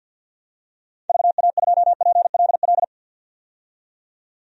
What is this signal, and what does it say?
Morse code, heard as a keyed tone, about 51 wpm, 700 Hz. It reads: VA2PLL
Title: DXCC-100 Call Signs - 6 Characters - 50wpm